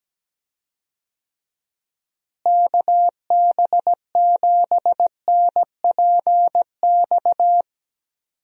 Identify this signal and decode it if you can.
Morse code, heard as a keyed tone, about 17 wpm, 700 Hz. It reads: KB7NPX
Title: USA Call Signs - 6 Characters - Repeated 3x - 17wpm